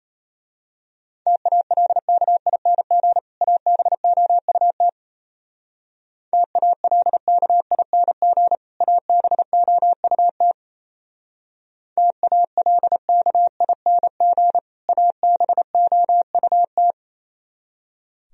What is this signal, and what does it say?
Morse code, heard as a keyed tone, about 32 wpm, 690 Hz. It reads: TALKING ABOUT TALKING ABOUT TALKING ABOUT
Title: Speed-Racing - Sets of 2 Words 28wpm